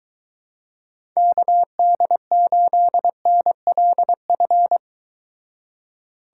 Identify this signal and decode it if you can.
Morse code, heard as a keyed tone, about 23 wpm, 700 Hz. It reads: KD8NLF